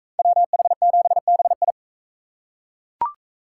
Morse code, heard as a keyed tone, about 40 wpm, 700 Hz. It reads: WH7BI